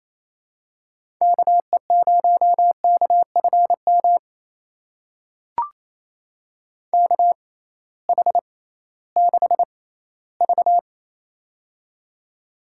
Morse code, heard as a keyed tone, about 28 wpm, spaced as for 16 wpm, 700 Hz. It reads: KE0KFM KH6V
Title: USA Call Signs - 6x Character Spacing - 28wpm